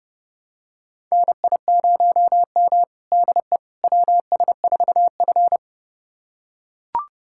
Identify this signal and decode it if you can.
Morse code, heard as a keyed tone, about 30 wpm, 700 Hz. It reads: NI0M DE WS4F
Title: USA Callsign DE Callsign - 4 Characters - 30wpm